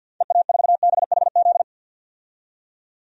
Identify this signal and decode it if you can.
Morse code, heard as a keyed tone, about 46 wpm, 700 Hz. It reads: EA4BHZ
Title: DXCC-100 Call Signs - 6 Characters - 50wpm